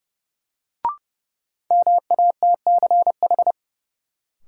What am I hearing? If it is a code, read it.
Morse code, heard as a keyed tone, about 30 wpm, 700 Hz. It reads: MATCH
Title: Sentences from Top 1600 Words - New Word Drill - 30wpm